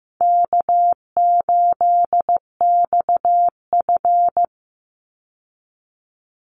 Morse code, heard as a keyed tone, about 15 wpm, 700 Hz. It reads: K8XF
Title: Call Signs Repeated 3x - 15wpm